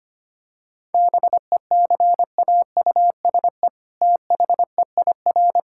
Morse code, heard as a keyed tone, about 25 wpm, 700 Hz. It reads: BECAUSE THEIR
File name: practice / 25wpm / All-Age Appropriate Jokes - 25wpm